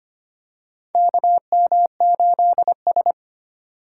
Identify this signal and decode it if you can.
Morse code, heard as a keyed tone, about 25 wpm, 700 Hz. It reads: KM8S
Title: USA Call Signs - 4 Characters - 2x Character Spacing - 25wpm